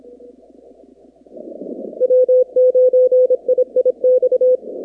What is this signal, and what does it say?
Morse code, heard as a keyed tone, about 26 wpm, 510 Hz. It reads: W9IIX
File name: contests / 2019-cw-wpx-kt5j-k5pi- -2330 0-44